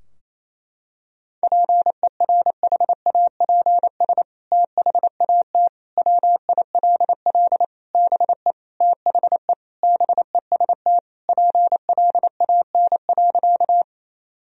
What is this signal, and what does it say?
Morse code, heard as a keyed tone, about 28 wpm, 700 Hz. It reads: PERHAPS THAT WILL BE THE BEST PLAN.
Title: Sentences from Top 1000 Words - Review Part 2 - 28wpm